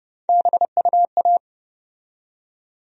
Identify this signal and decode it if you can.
Morse code, heard as a keyed tone, about 30 wpm, 700 Hz. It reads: BUA
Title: USA Call Sign Postfixes - 3 Letters 30wpm